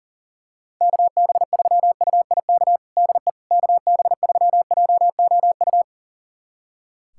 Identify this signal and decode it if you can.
Morse code, heard as a keyed tone, about 40 wpm, 700 Hz. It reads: KB3UIK DE KB3JOU
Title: USA Callsign DE Callsign - 6 Characters - 40wpm